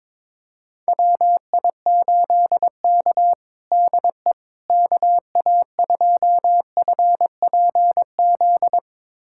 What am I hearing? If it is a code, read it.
Morse code, heard as a keyed tone, about 22 wpm, 700 Hz. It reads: WI8K DE KA2FPZ
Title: USA Callsign DE Callsign 22wpm